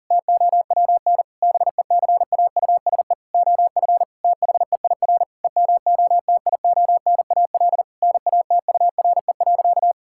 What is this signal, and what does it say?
Morse code, heard as a keyed tone, about 40 wpm, 700 Hz. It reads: TOWN BECAUSE OF THEIR EMOTIONAL NATURE.
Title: Sentences from Top 1000 Words - Review Part 5 - 40wpm